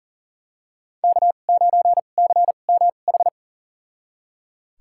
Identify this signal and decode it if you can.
Morse code, heard as a keyed tone, about 40 wpm, spaced as for 29 wpm, 690 Hz. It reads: K9CMH
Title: USA Call Signs - 2x Character Spacing - 40wpm